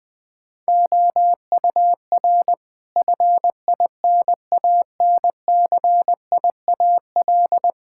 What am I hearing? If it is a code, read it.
Morse code, heard as a keyed tone, about 20 wpm, 700 Hz. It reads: OUR FINANCIAL
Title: Sentences from Top 1000 Words - Review Part 5 - 20wpm